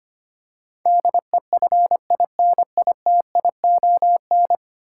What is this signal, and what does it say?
Morse code, heard as a keyed tone, about 25 wpm, 700 Hz. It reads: DEFINITION
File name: 10 Letter Words - First 3 Letters Given - 25wpm